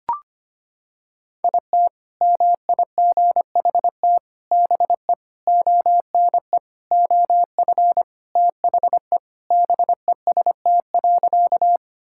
Morse code, heard as a keyed tone, about 25 wpm, 700 Hz. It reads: IT MIGHT BE ONE OF THE BEST.
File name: practice / 25wpm / Sentences with 4-Word Phrases - 25wpm